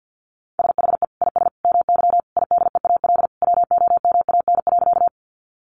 Morse code, heard as a keyed tone, about 50 wpm, 700 Hz. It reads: SHE IS MY IDEAL WOMAN.
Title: Sentences from Top 1400 Words 50wpm